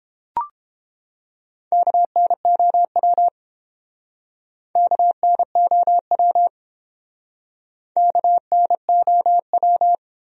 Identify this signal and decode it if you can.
Morse code, heard as a keyed tone, about 29 wpm, 700 Hz. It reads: KNOW KNOW KNOW